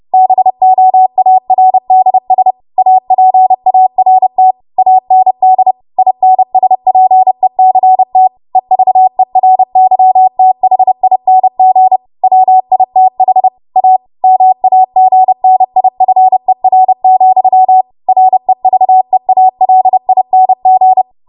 Morse code, heard as a keyed tone, about 30 wpm, 750 Hz. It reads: BOARDS APART AND INSPECT EVERYTHING WITH A MAGNIFER, REVEALING
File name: Archive / 30 WPM / 30WPM